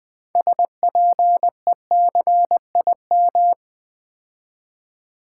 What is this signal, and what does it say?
Morse code, heard as a keyed tone, about 20 wpm, 690 Hz. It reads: SPECIM
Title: Build A Word - 3 Letters At A Time - 8-Letter Words - 20wpm